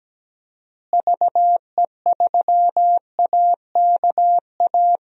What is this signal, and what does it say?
Morse code, heard as a keyed tone, about 17 wpm, 700 Hz. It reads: VE3AKA
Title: DXCC-100 Call Signs - 6 Characters - Repeated 3x - 17wpm